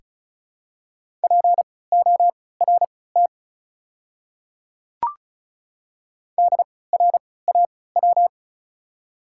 Morse code, heard as a keyed tone, about 36 wpm, spaced as for 20 wpm, 700 Hz. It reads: PORT DRAW
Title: 3x Character Spacing - Top 300 Words - 35wpm